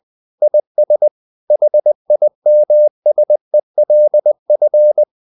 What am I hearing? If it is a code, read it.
Morse code, heard as a keyed tone, about 20 wpm, 600 Hz. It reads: IS HIMSELF
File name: the old man and the sea / the old man and the sea part 1 20wpm